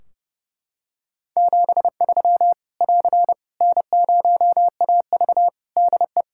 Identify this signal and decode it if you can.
Morse code, heard as a keyed tone, about 30 wpm, 700 Hz. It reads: 73 <AR> N0AV DE
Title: Practice QSO - Exchange 4 - Ending - 30wpm